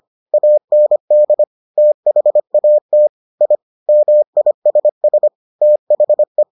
Morse code, heard as a keyed tone, about 25 wpm, 600 Hz. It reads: AND THAT I MISS THE